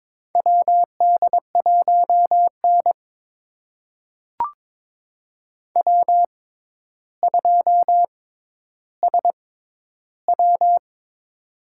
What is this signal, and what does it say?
Morse code, heard as a keyed tone, about 22 wpm, spaced as for 11 wpm, 700 Hz. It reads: WD1N W2SW
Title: USA Call Signs - 4 Characters - 6x Character Spacing - 22wpm